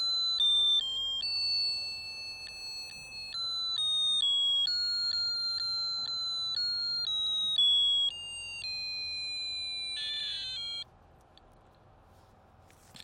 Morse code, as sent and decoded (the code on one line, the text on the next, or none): none